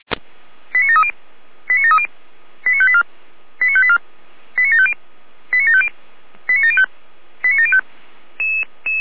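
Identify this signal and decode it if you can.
Five-tone selective call (ZVEI), heard as quick runs of five tones, tones of about 70 ms. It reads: 89730 89730 89584 89584 89750 89750 89705 89705
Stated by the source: Five-tone ZVEI selective call received by SDR.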